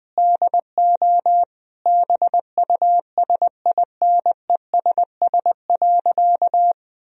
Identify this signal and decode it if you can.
Morse code, heard as a keyed tone, about 20 wpm, 700 Hz. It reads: DO BUSINESS.
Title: Sentences from Top 1000 Words - Review Part 3 - 20wpm